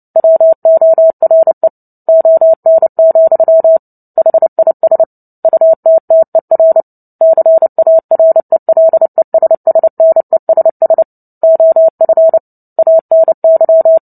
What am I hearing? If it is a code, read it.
Morse code, heard as a keyed tone, about 30 wpm, 650 Hz. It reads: WORE ON, HIS UTTER CARELESSNESS OF ANY
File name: Book Two / 03-Chapter 3 - 29 WPM